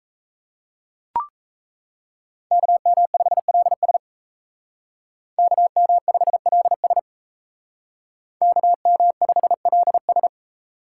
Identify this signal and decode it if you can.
Morse code, heard as a keyed tone, about 36 wpm, 700 Hz. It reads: KM5LS KM5LS KM5LS